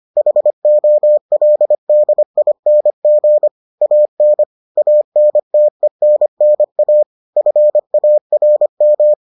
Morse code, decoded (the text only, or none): HOLDING AN ANTENNA FARM